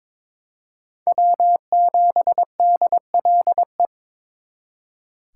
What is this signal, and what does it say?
Morse code, heard as a keyed tone, about 26 wpm, 700 Hz. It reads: W7DLE
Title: Speed Racing - USA Call Signs 22wpm